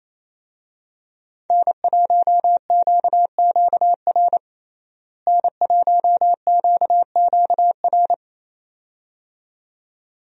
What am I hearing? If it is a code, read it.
Morse code, heard as a keyed tone, about 28 wpm, 700 Hz. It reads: N1QQR N1QQR
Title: USA Call Signs - 5 Characters - Repeated 2x - 28wpm